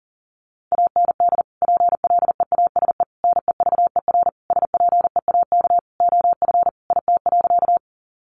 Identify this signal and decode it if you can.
Morse code, heard as a keyed tone, about 40 wpm, 700 Hz. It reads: AND PLEASE NEVER SPEAK OF IT.
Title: Sentences from Top 1500 Words 40wpm